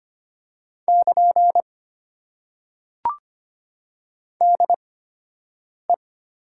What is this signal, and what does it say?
Morse code, heard as a keyed tone, about 25 wpm, spaced as for 7 wpm, 700 Hz. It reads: <KN> DE
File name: Top 90 QSO Elements - 8x Character Spacing - 25wpm